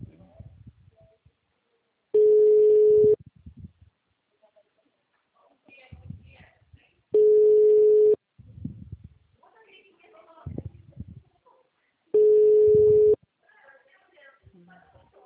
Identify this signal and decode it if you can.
Ringback tone (Europe, 425 Hz), with speech.
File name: out- -2010- - - .317